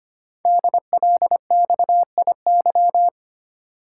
Morse code, heard as a keyed tone, about 25 wpm, 700 Hz. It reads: DLXIY
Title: Code Groups - 5 Letters - 25wpm